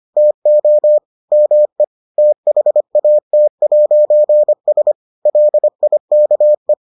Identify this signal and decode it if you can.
Morse code, heard as a keyed tone, about 25 wpm, 600 Hz. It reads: TO ME THAT'S LIKE